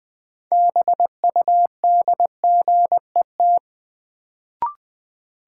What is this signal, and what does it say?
Morse code, heard as a keyed tone, about 21 wpm, 700 Hz. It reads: BUDGET
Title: Sentences from Top 1200 Words - New Word Drill - 20wpm